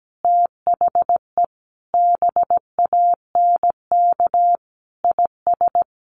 Morse code, heard as a keyed tone, about 17 wpm, 700 Hz. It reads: THE BANK IS
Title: Sentences from Top 1500 Words 17wpm